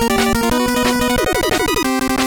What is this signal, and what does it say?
There is no code.